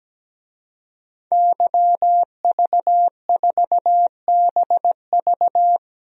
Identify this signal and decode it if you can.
Morse code, heard as a keyed tone, about 17 wpm, 700 Hz. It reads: YV4BV
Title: DXCC-100 Call Signs - 5 Characters - Repeated 3x - 17wpm